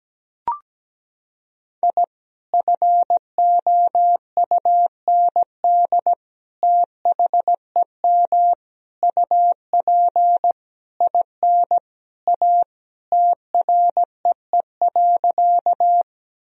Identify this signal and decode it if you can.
Morse code, heard as a keyed tone, about 17 wpm, 700 Hz. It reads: I FOUND THEM UP IN A TREE.
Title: Sentences from Top 1000 Words - Review Part 1 - 17wpm